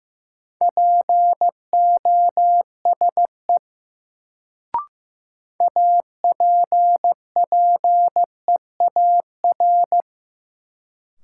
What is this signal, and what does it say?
Morse code, heard as a keyed tone, about 15 wpm, 700 Hz. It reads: POSE APPEAR
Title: Sentences from Top 1000 Words - Review Part 2 - New Word Drill - 15wpm